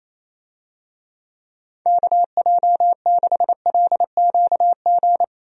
Morse code, heard as a keyed tone, about 28 wpm, 700 Hz. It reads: KJ6LQG
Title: USA Call Signs - 6 Characters - 28wpm